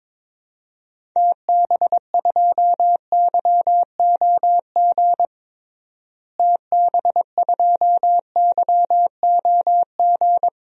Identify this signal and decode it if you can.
Morse code, heard as a keyed tone, about 22 wpm, 700 Hz. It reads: TB2YOG TB2YOG
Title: DXCC-100 Call Signs - 6 Characters - Repeated 2x - 22wpm